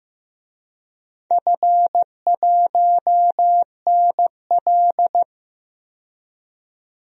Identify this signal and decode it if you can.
Morse code, heard as a keyed tone, about 15 wpm, 700 Hz. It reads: F1NL
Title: DXCC-100 Call Signs - 4 Characters - 15wpm